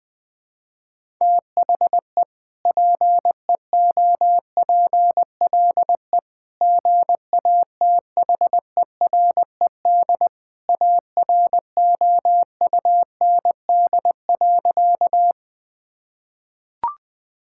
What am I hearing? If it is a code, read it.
Morse code, heard as a keyed tone, about 20 wpm, 700 Hz. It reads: THE PEOPLE GATHERED AROUND.
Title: Sentences with 2-Word Phrases - 20wpm